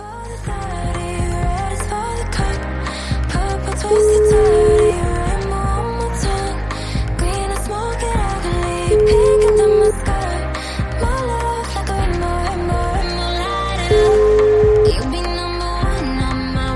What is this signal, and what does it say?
Ringback tone (Europe, 425 Hz), with noise.